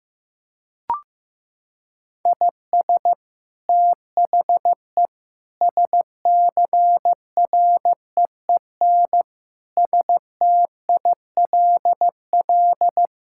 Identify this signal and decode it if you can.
Morse code, heard as a keyed tone, about 15 wpm, 700 Hz. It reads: IS THE SCREEN STILL
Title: Sentences from Top 1000 Words - Review Part 6 - 15wpm